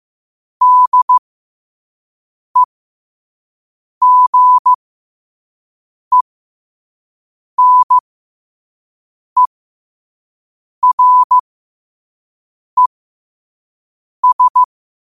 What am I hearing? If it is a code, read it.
Morse code, heard as a keyed tone, about 15 wpm, spaced as for 5 wpm, 1000 Hz. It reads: DEGENERES
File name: qotdfiles / 5wpm